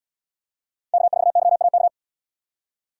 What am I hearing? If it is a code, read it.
Morse code, heard as a keyed tone, about 73 wpm, 700 Hz. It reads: K4YAK